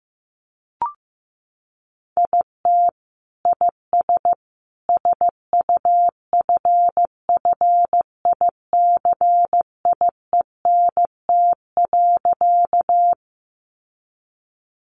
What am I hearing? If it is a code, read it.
Morse code, heard as a keyed tone, about 15 wpm, 700 Hz. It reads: IT IS SUFFICIENT.
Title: Sentences from Top 1400 Words 15wpm